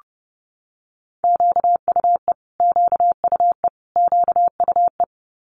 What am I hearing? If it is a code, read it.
Morse code, heard as a keyed tone, about 30 wpm, 700 Hz. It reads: QUE QUE QUE
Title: Multi-letter Phonemes Repeated 3x 30wpm